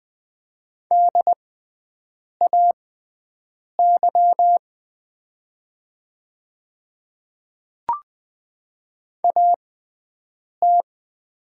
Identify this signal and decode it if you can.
Morse code, heard as a keyed tone, about 20 wpm, spaced as for 7 wpm, 690 Hz. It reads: DAY AT
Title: 6x Character Spacing - Top 100 Words - 20wpm